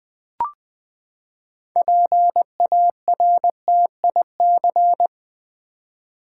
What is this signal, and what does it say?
Morse code, heard as a keyed tone, about 20 wpm, 700 Hz. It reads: PARTIC
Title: Build A Word - 6 Letters At A Time - 12-Letter Words - 20wpm